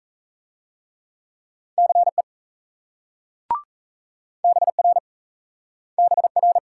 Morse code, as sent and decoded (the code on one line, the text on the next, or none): -.-. / -... .-. / -... .-.
CE BR BR